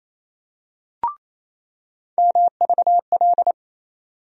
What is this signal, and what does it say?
Morse code, heard as a keyed tone, about 28 wpm, 700 Hz. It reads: MVL
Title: USA Call Sign Postfixes - 3 Letters 28wpm